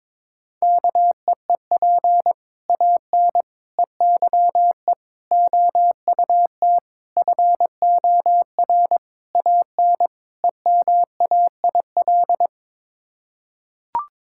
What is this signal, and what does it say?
Morse code, heard as a keyed tone, about 22 wpm, 700 Hz. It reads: KEEP AN EYE OUT FOR AN EMAIL
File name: practice / 22wpm / Sets of 7 Words - 22wpm